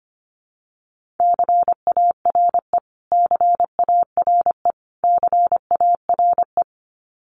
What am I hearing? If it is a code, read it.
Morse code, heard as a keyed tone, about 25 wpm, 700 Hz. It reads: CARE CARE CARE
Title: Top 500 Words Repeated 25wpm